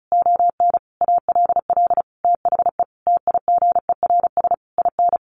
Morse code, heard as a keyed tone, about 34 wpm, 700 Hz. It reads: ON ALL THE TIGERS IN
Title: A Sound Of Thunder 35wpm